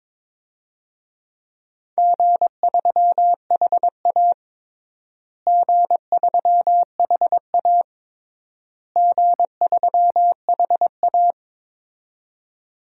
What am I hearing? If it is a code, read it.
Morse code, heard as a keyed tone, about 22 wpm, 700 Hz. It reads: G3HA G3HA G3HA
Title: DXCC-100 Call Signs - 4 Characters - Repeated 3x - 22wpm